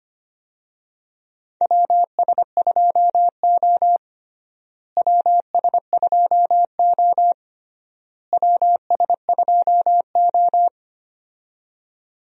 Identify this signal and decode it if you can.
Morse code, heard as a keyed tone, about 25 wpm, 700 Hz. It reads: WS2O WS2O WS2O
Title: USA Call Signs - 4 Characters - Repeated 3x - 25wpm